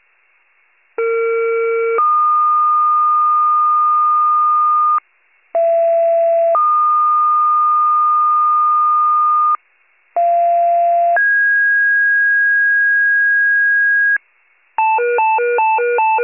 Two-tone page, heard as a one-second tone, then a three-second tone, about 460 then 1220 Hz, about 670 then 1220 Hz, about 690 then 1650 Hz. It is heard from 1.0 to 14.2 s.